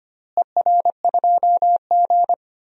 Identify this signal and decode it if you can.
Morse code, heard as a keyed tone, about 25 wpm, 700 Hz. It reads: ER2G
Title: DXCC-100 Call Signs - 4 Characters - Repeated 2x - 25wpm